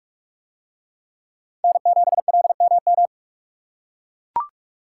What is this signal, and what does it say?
Morse code, heard as a keyed tone, about 43 wpm, 700 Hz. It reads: N7LMM